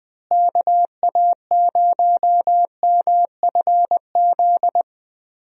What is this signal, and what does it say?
Morse code, heard as a keyed tone, about 20 wpm, 700 Hz. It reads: KA0MFZ
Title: USA Call Signs - 6 Characters - Repeated 3x - 20wpm